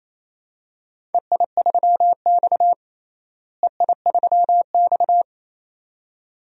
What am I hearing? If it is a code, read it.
Morse code, heard as a keyed tone, about 28 wpm, 700 Hz. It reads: EI3X EI3X
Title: DXCC-100 Call Signs - 4 Characters - Repeated 2x - 28wpm